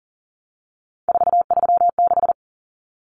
Morse code, heard as a keyed tone, about 40 wpm, 700 Hz. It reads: 436
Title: Common Cut Numbers - 3-Digit 40wpm